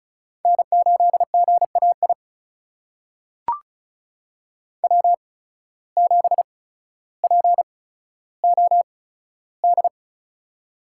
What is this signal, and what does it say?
Morse code, heard as a keyed tone, about 34 wpm, spaced as for 15 wpm, 700 Hz. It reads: N8GAI W7POD
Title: USA Call Signs - 5 Characters - 8x Character Spacing - 35wpm